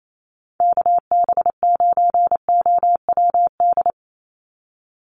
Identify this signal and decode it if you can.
Morse code, heard as a keyed tone, about 28 wpm, 700 Hz. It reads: KB9OWD
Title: Call Signs - 28wpm